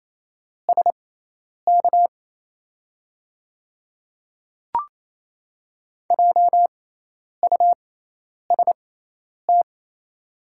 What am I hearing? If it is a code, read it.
Morse code, heard as a keyed tone, about 28 wpm, spaced as for 10 wpm, 700 Hz. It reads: SK JUST
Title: 6x Character Spacing - Top 200 Words - 28wpm